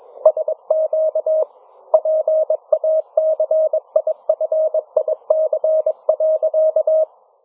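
Morse code, heard as a keyed tone, about 22 wpm, 630 Hz. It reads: SQ PACIFIC.
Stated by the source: Morse challenge from NZ Net newsletter; NZ Net, national amateur radio net of New Zealand, meets on 3535.0 kHz.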